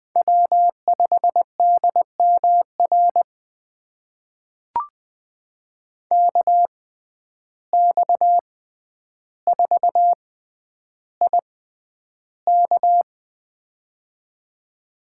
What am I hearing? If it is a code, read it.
Morse code, heard as a keyed tone, about 20 wpm, spaced as for 10 wpm, 700 Hz. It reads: W5DMR KX4IK
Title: USA Call Signs - 5 Characters - 6x Character Spacing - 20wpm